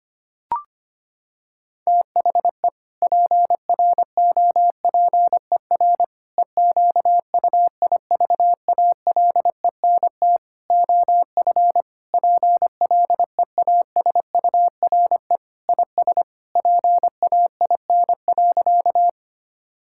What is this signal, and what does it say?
Morse code, heard as a keyed tone, about 25 wpm, 700 Hz. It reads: THE PROPER EQUIVALENT OF PLEASURE IS PAIN.
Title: Sentences from Top 1600 Words 25wpm